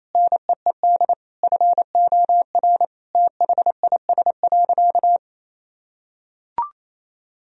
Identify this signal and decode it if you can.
Morse code, heard as a keyed tone, about 28 wpm, 700 Hz. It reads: NEED FOR THIS.
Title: Sentences from Top 1000 Words - Review Part 1 - 28wpm